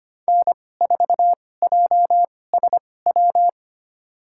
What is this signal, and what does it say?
Morse code, heard as a keyed tone, about 25 wpm, spaced as for 18 wpm, 700 Hz. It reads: N4JSW
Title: USA Call Signs - 5 Characters - 2x Character Spacing - 25wpm